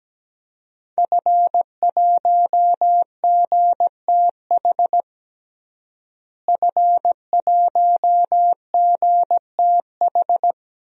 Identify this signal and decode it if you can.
Morse code, heard as a keyed tone, about 17 wpm, 700 Hz. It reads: F1GTH F1GTH
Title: DXCC-100 Call Signs - 5 Characters - Repeated 3x - 17wpm